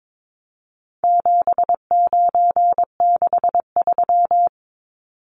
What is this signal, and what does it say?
Morse code, heard as a keyed tone, about 22 wpm, 700 Hz. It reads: 7963
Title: Numbers - 4-Digit 22wpm